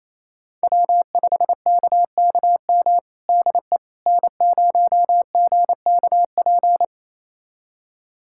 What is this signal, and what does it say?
Morse code, heard as a keyed tone, about 28 wpm, 700 Hz. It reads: W5KKM DE N0GKP
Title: USA Callsign DE Callsign - 5 Characters - 28wpm